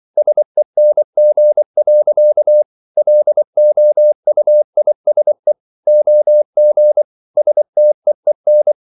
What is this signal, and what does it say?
Morse code, heard as a keyed tone, about 24 wpm, 600 Hz. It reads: SENG. LOUISE OG STEEN
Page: Ferie 24 wpm